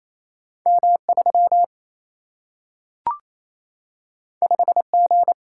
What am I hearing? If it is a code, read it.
Morse code, heard as a keyed tone, about 28 wpm, 700 Hz. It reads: M3 5G
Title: Single-Letter Single-Number 28wpm